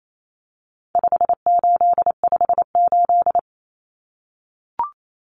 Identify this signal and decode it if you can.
Morse code, heard as a keyed tone, about 28 wpm, 700 Hz. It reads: 5858
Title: Numbers - 4-Digit 28wpm